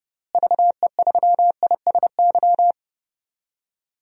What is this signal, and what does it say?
Morse code, heard as a keyed tone, about 30 wpm, 700 Hz. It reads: VE3ISY
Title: DXCC-100 Call Signs - 6 Characters - Repeated 2x - 30wpm